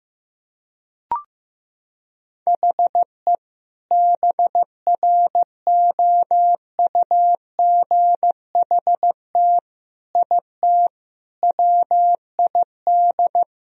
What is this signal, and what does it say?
Morse code, heard as a keyed tone, about 15 wpm, 700 Hz. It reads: HE BROUGHT IT WID
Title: Sentences from Top 500 Words and Abbreviations 15wpm